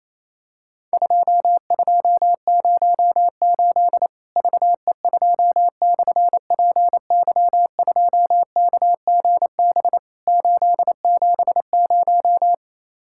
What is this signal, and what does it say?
Morse code, heard as a keyed tone, about 27 wpm, 700 Hz. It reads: 2208 VE2/PY2KGB 870
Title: Worked All Europe DX Contest - QTC Exchange - 28wpm